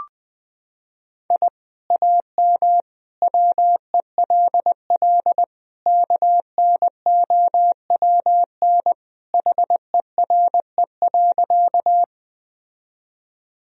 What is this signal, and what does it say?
Morse code, heard as a keyed tone, about 20 wpm, 700 Hz. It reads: I AM WELL KNOWN HERE.
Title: Sentences from Top 1000 Words - Review Part 3 - 20wpm